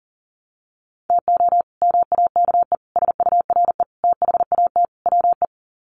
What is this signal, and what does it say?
Morse code, heard as a keyed tone, about 40 wpm, 700 Hz. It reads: TO MAKE SURE THAT WE
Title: Sets of 5 Words Encore 40wpm